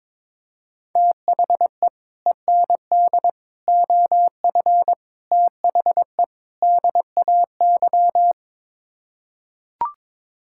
Morse code, heard as a keyed tone, about 22 wpm, 700 Hz. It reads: THE END OF THE DAY
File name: practice / 22wpm / Build A Phrase - 5-Word Phrases - 22wpm